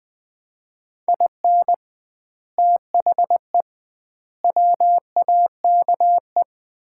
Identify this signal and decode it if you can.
Morse code, heard as a keyed tone, about 20 wpm, 700 Hz. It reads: IN THE WAKE
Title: 2x Word-Spacing - Sets of 4 Words - 20wpm